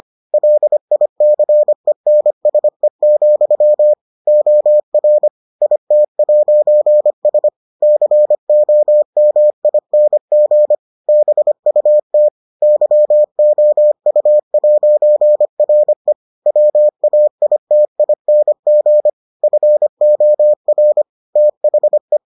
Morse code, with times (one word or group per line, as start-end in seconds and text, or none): 0.3-3.9 LICENSE,
4.3-5.3 OR
5.6-7.5 IT'S
7.8-10.8 COMING
11.1-12.3 BUT
12.6-16.1 YOU'RE
16.5-19.1 WAITING
19.4-21.0 FOR
21.4-22.2 THE